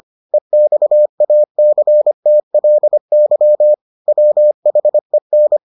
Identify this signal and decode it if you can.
Morse code, heard as a keyed tone, about 25 wpm, 600 Hz. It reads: EXACTLY WHEN